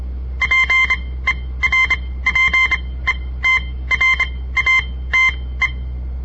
Morse code, heard as a keyed tone, about 26 wpm, spaced as for 17 wpm, 2100 Hz. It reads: PERPETRATE
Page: On the air Morse code practice from the Southern Montana Amatuer Radio Association; all code, no read back